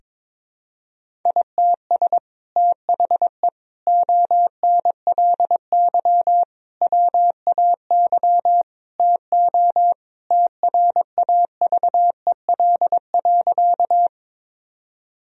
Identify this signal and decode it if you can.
Morse code, heard as a keyed tone, about 22 wpm, 700 Hz. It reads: ITS THE ONLY WAY TO TRAVEL.
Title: Sentences from Top 1000 Words - Review Part 2 - 22wpm